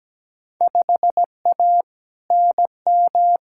Morse code, heard as a keyed tone, about 17 wpm, 700 Hz. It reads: 5A NM
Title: Field Day Exchanges - 17wpm